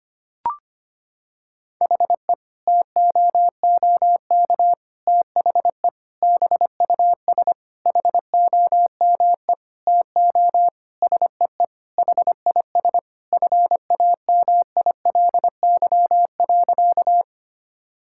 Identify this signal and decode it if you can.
Morse code, heard as a keyed tone, about 25 wpm, 700 Hz. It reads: HE TOOK THE BUS HOME TO SEE HIS FAMILY.
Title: Sentences from Top 1000 Words - Review Part 6 - 25wpm